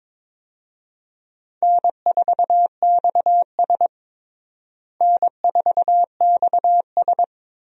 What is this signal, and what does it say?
Morse code, heard as a keyed tone, about 22 wpm, 700 Hz. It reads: N4XS N4XS
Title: USA Call Signs - 4 Characters - Repeated 2x - 22wpm